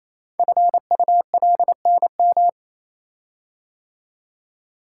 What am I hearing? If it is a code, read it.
Morse code, heard as a keyed tone, about 28 wpm, 700 Hz. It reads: FULNM
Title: Code Groups - 5 Letters - 28wpm